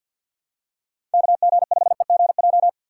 Morse code, heard as a keyed tone, about 50 wpm, 700 Hz. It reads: KG5EGJ
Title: USA Call Signs - 6 Characters - 4x Character Spacing - 50wpm